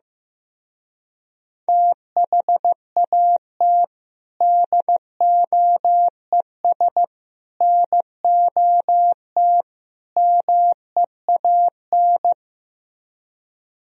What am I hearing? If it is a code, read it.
Morse code, heard as a keyed tone, about 15 wpm, 700 Hz. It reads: THAT DOES NOT MEAN
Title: Speed-Racing - Sets of 4 Words 15wpm